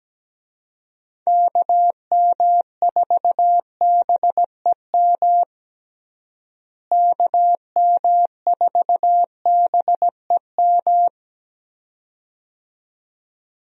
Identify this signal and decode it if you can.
Morse code, heard as a keyed tone, about 17 wpm, 700 Hz. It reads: KM4BEM KM4BEM
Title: USA Call Signs - 6 Characters - Repeated 2x - 17wpm